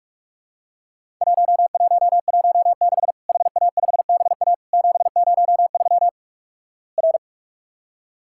Morse code, heard as a keyed tone, about 46 wpm, 700 Hz. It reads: 1116 HA5BA 703 R
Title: Worked All Europe DX Contest - QTC Exchange - 45wpm